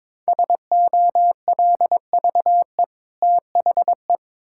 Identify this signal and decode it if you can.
Morse code, heard as a keyed tone, about 22 wpm, 700 Hz. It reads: SOLVE THE
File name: practice / 22wpm / Easy Sentences with Hints - 22wpm